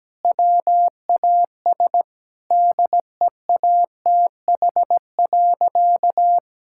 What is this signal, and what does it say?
Morse code, heard as a keyed tone, about 17 wpm, 700 Hz. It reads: WAS DEATH.
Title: Sentences from Top 1600 Words 17wpm